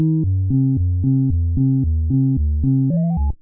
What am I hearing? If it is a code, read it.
There is no code.